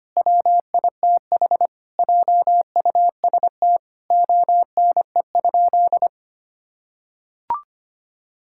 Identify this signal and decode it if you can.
Morse code, heard as a keyed tone, about 25 wpm, 700 Hz. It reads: WITH JUST ONE?
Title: Sentences from Top 1000 Words - Review Part 6 - 25wpm